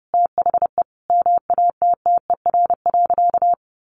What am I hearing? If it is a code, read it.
Morse code, heard as a keyed tone, about 30 wpm, 700 Hz. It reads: THE MATTER.
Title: Sentences from Top 1300 Words 30wpm